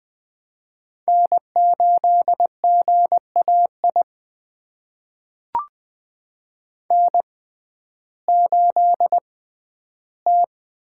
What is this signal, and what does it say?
Morse code, heard as a keyed tone, about 20 wpm, spaced as for 12 wpm, 700 Hz. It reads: N8GAI N8T
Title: USA Call Signs - 5 Characters - 6x Character Spacing - 20wpm